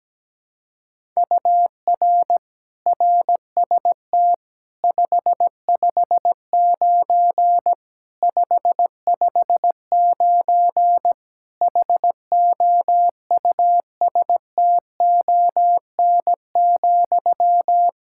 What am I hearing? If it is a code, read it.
Morse code, heard as a keyed tone, about 17 wpm, 700 Hz. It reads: UR RST 559 559 HOUSTON,
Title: K1USN SST QSO Practice 17wpm